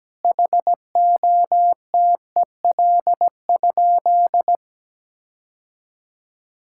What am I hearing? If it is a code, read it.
Morse code, heard as a keyed tone, about 17 wpm, 700 Hz. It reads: HOTEL?
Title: Sentences from Top 1600 Words 17wpm